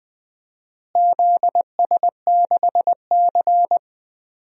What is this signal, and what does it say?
Morse code, heard as a keyed tone, about 20 wpm, 700 Hz. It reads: ZS6C
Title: DXCC-100 Call Signs - 4 Characters - Repeated 3x - 20wpm